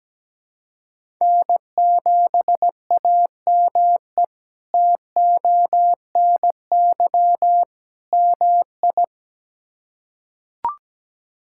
Morse code, heard as a keyed tone, about 17 wpm, 700 Hz. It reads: N7AME TONY MI
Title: K1USN SST QSO Practice - 17wpm